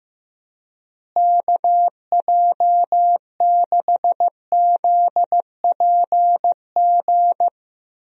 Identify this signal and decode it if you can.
Morse code, heard as a keyed tone, about 15 wpm, 700 Hz. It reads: KJ6ZPG
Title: USA Call Signs - 6 Characters - Repeated 2x - 15wpm